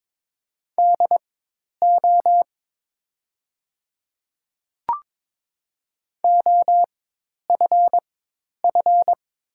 Morse code, heard as a keyed tone, about 22 wpm, spaced as for 10 wpm, 700 Hz. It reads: DO OFF
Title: 4x Character Spacing - Top 200 Words - 22wpm